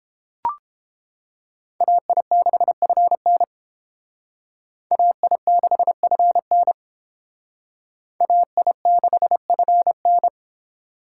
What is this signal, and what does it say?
Morse code, heard as a keyed tone, about 28 wpm, 700 Hz. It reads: AI6FN AI6FN AI6FN